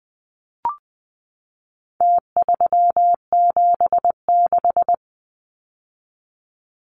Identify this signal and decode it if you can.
Morse code, heard as a keyed tone, about 20 wpm, 700 Hz. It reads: T376
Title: Numbers - Contest Style - 4-Digit 20wpm